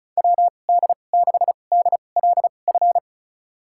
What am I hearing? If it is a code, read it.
Morse code, heard as a keyed tone, about 34 wpm, spaced as for 25 wpm, 700 Hz. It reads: WD6DLF
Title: USA Call Signs - 6 Characters - 2x Character Spacing - 35wpm